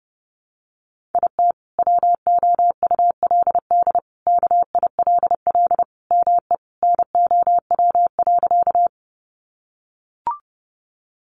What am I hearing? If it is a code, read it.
Morse code, heard as a keyed tone, about 30 wpm, 700 Hz. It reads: IT WOULD KILL ME NOW.
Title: Sentences from Top 1400 Words 30wpm